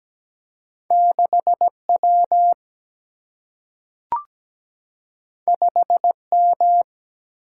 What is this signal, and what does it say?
Morse code, heard as a keyed tone, about 17 wpm, 700 Hz. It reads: 6W 5M
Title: Single-Letter Single-Number 17wpm